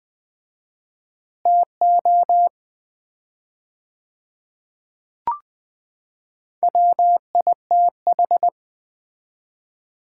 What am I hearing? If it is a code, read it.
Morse code, heard as a keyed tone, about 20 wpm, 700 Hz. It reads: TO WITH
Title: 4x Word-Spacing - Sets of 2 Words - 20wpm